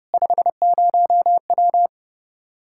Morse code, heard as a keyed tone, about 30 wpm, 700 Hz. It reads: 50W
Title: Practice QSO - Exchange 2 - Info - 30wpm